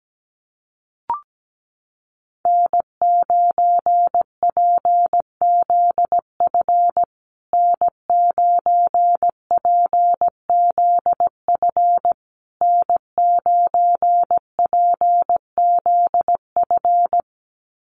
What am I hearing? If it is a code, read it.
Morse code, heard as a keyed tone, about 17 wpm, 700 Hz. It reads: N9PZF N9PZF N9PZF
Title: USA Call Signs Repeated 3x 17wpm